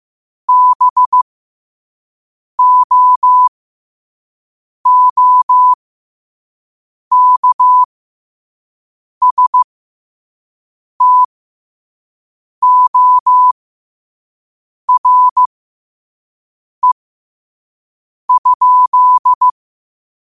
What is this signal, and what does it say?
Morse code, heard as a keyed tone, about 15 wpm, spaced as for 5 wpm, 1000 Hz. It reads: BOOKSTORE?